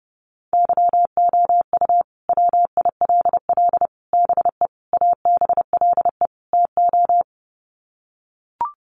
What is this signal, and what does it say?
Morse code, heard as a keyed tone, about 30 wpm, 700 Hz. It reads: YOU WILL BE ABLE TO
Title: Sets of 5 Words Encore2 30wpm